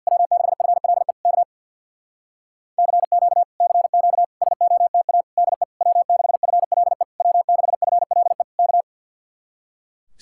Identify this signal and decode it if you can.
Morse code, heard as a keyed tone, about 66 wpm, 700 Hz. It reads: W6FLE K CQ CQ SOTA DE W6FLE W6FLE K